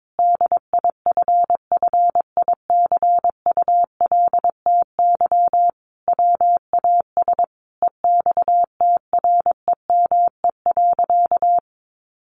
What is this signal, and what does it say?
Morse code, heard as a keyed tone, about 22 wpm, 700 Hz. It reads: DIFFICULTY WAS EXTREME.
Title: Sentences from Top 1500 Words 22wpm